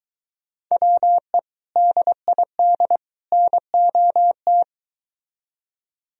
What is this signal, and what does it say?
Morse code, heard as a keyed tone, about 22 wpm, 700 Hz. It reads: WE DID NOT